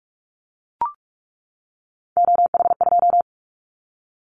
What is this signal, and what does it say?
Morse code, heard as a keyed tone, about 44 wpm, 700 Hz. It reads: OH2
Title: Callsign Prefixes for Top 50 DXCC - 45wpm